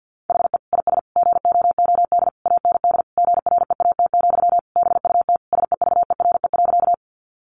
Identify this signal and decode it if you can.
Morse code, heard as a keyed tone, about 50 wpm, 700 Hz. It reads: HE IS GOOD AND GREAT, BUT SEVERE.
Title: Sentences from Top 1400 Words 50wpm